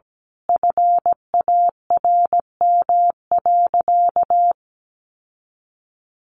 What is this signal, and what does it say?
Morse code, heard as a keyed tone, about 17 wpm, 700 Hz. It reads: FARM.
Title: Sentences from Top 1200 Words 17wpm